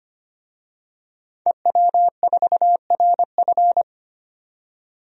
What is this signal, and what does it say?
Morse code, heard as a keyed tone, about 25 wpm, 700 Hz. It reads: EW4RF
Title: DXCC-100 Call Signs - 5 Characters - 25wpm